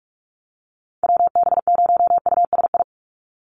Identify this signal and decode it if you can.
Morse code, heard as a keyed tone, about 46 wpm, 700 Hz. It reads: WB0USI
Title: NA SOTA Activator Callsigns 45wpm